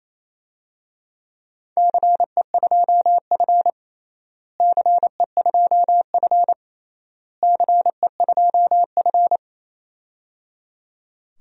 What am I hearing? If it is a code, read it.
Morse code, heard as a keyed tone, about 28 wpm, 700 Hz. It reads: CE2F CE2F CE2F
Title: DXCC-100 Call Signs - 4 Characters - Repeated 3x - 28wpm